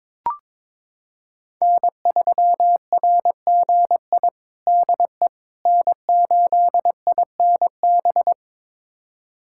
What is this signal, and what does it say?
Morse code, heard as a keyed tone, about 21 wpm, 700 Hz. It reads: N3RGI DE N8INB
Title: USA Callsign DE Callsign - 5 Characters - 22wpm